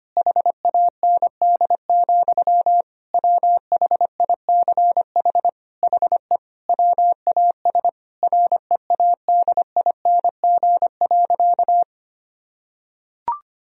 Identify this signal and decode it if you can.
Morse code, heard as a keyed tone, about 25 wpm, 700 Hz. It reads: HAND, WHICH HE WAS READING.
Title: Sentences from Top 1000 Words - Review Part 3 - 25wpm